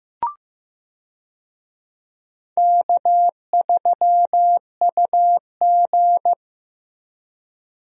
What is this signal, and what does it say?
Morse code, heard as a keyed tone, about 16 wpm, 700 Hz. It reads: K3UG
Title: USA Call Signs - 4 Characters - Repeated 3x - 15wpm